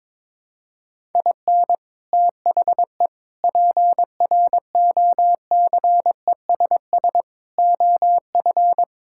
Morse code, heard as a keyed tone, about 22 wpm, 700 Hz. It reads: IN THE PROCESS OF
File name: practice / 22wpm / Sentences with 4-Word Phrases - 22wpm